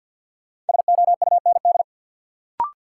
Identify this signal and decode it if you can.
Morse code, heard as a keyed tone, about 53 wpm, 700 Hz. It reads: SOUND
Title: Sentences from Top 900 Words - New Word Drill - 50wpm